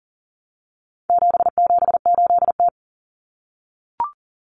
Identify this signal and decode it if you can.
Morse code, heard as a keyed tone, about 38 wpm, 700 Hz. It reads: 778T
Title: Common Cut Numbers - 4-Digit 40wpm